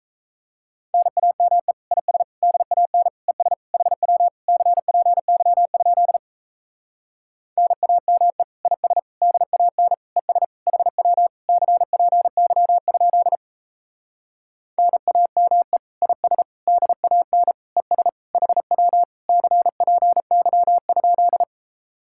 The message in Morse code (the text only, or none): NAME IS DAN ES HW CPY? NAME IS DAN ES HW CPY? NAME IS DAN ES HW CPY?